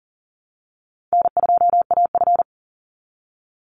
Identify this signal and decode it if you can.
Morse code, heard as a keyed tone, about 40 wpm, 700 Hz. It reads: N2AF
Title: NA SOTA Chaser Callsigns 40wpm